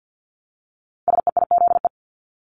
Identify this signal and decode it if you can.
Morse code, heard as a keyed tone, about 50 wpm, 700 Hz. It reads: SEIZE
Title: Multi-letter Phonemes Repeated 3x 50wpm